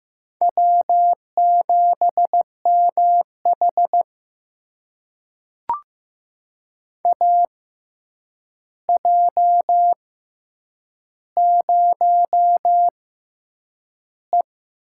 Morse code, heard as a keyed tone, about 15 wpm, spaced as for 8 wpm, 700 Hz. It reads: W7MH AJ0E
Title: USA Call Signs - 4 Characters - 6x Character Spacing - 15wpm